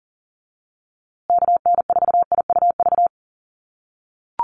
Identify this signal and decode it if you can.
Morse code, heard as a keyed tone, about 40 wpm, 700 Hz. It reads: KN4IUV
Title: USA Call Signs Repeated 3x 40wpm